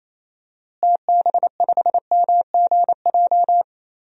Morse code, heard as a keyed tone, about 27 wpm, 700 Hz. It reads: TB5MGJ
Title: DXCC-100 Call Signs - 6 Characters - Repeated 3x - 28wpm